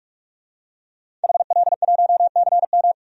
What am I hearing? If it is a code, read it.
Morse code, heard as a keyed tone, about 41 wpm, 700 Hz. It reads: HL1CM
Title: DXCC-100 Call Signs - 5 Characters - 45wpm